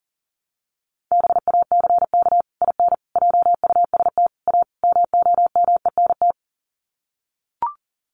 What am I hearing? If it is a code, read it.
Morse code, heard as a keyed tone, about 40 wpm, 700 Hz. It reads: BACK IN JUST A MOMENT
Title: Sets of 5 Words Encore 40wpm